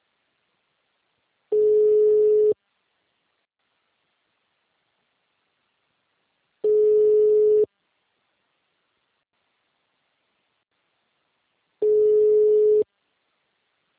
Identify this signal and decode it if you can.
Ringback tone (Europe, 425 Hz).